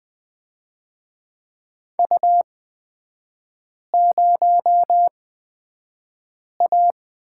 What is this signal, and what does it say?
Morse code, heard as a keyed tone, about 20 wpm, spaced as for 5 wpm, 700 Hz. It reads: U0A
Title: 224 Lesson 36e - BT - Three Characters 20wpm